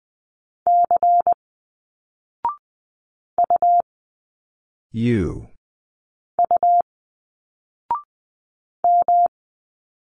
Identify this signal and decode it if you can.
Morse code, heard as a keyed tone, about 20 wpm, 700 Hz. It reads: C U U M
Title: Single Letters 20wpm